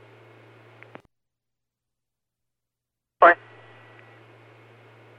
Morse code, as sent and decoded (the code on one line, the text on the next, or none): none